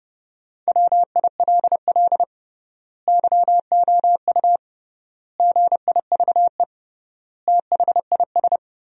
Morse code, heard as a keyed tone, about 30 wpm, 700 Hz. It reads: WILL YOU GIVE THIS
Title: 3x Word-Spacing - Sentences from Top 200 Words - 30wpm